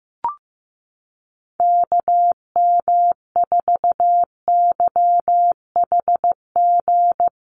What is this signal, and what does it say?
Morse code, heard as a keyed tone, about 15 wpm, 700 Hz. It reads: KM4YHG
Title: USA Call Signs Repeated 2x 15wpm